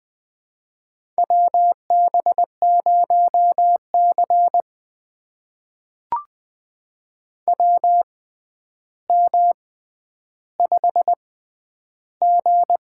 Morse code, heard as a keyed tone, about 20 wpm, spaced as for 10 wpm, 700 Hz. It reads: WB0C WM5G
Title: USA Call Signs - 4 Characters - 6x Character Spacing - 20wpm